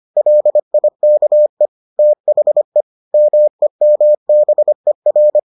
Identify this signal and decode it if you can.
Morse code, heard as a keyed tone, about 25 wpm, 600 Hz. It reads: LIKE THE MEMBER